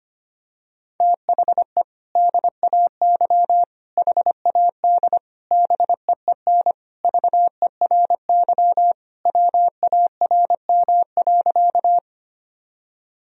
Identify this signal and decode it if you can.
Morse code, heard as a keyed tone, about 25 wpm, 700 Hz. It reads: THE DAY HAD BEEN VERY WARM.
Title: Sentences from Top 1000 Words - Review Part 2 - 25wpm